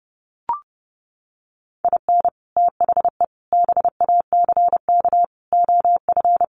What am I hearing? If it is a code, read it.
Morse code, heard as a keyed tone, about 30 wpm, 700 Hz. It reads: IN THE BACK OF
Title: Sets of 5 Words 30wpm